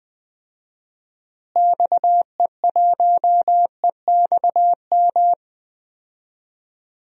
Morse code, heard as a keyed tone, about 20 wpm, 700 Hz. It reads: XE1EXM
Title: DXCC-100 Call Signs - 6 Characters - Repeated 3x - 20wpm